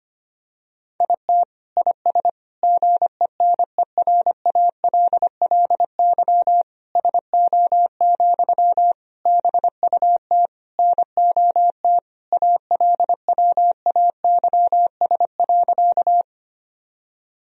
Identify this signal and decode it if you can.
Morse code, heard as a keyed tone, about 25 wpm, 700 Hz. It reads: IT IS GENERALLY SO, BUT NOT ALWAYS.
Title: Sentences from Top 1000 Words - Review Part 4 - 25wpm